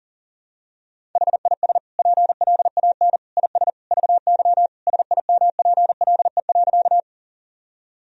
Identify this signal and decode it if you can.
Morse code, heard as a keyed tone, about 40 wpm, 700 Hz. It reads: HIS PLAN IS VY SIMPLE.
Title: Sentences from Top 500 Words and Abbreviations 40wpm